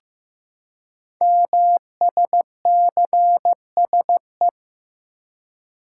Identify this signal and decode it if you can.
Morse code, heard as a keyed tone, about 15 wpm, 700 Hz. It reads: MSCSE
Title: Code Groups - 5 Letter:Numbers - 15wpm